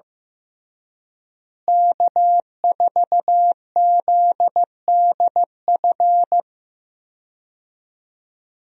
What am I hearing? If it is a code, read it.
Morse code, heard as a keyed tone, about 15 wpm, 700 Hz. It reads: K4ZDF
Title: USA Call Signs - 5 Characters - Repeated 3x - 15wpm